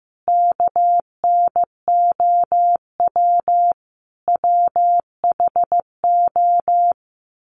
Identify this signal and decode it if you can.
Morse code, heard as a keyed tone, about 15 wpm, 700 Hz. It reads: KNOW WHO
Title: Sets of 5 Words 15wpm